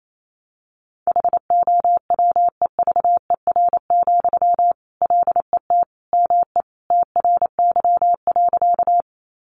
Morse code, heard as a keyed tone, about 28 wpm, 700 Hz. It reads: HOWEVER, LET ME TRY.
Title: Sentences from Top 600 Words 28wpm